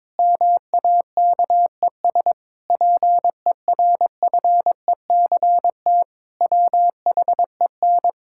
Morse code, heard as a keyed tone, about 21 wpm, 700 Hz. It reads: MAKES PERFECT WHEN
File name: Difficult Sentences with Hints - 22wpm